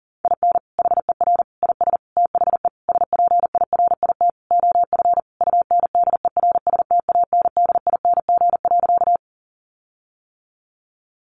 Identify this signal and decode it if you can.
Morse code, heard as a keyed tone, about 37 wpm, 700 Hz. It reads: IN HER IS THE SPIRIT OF UNDERSTANDING.
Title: Sentences from Top 1400 Words 40wpm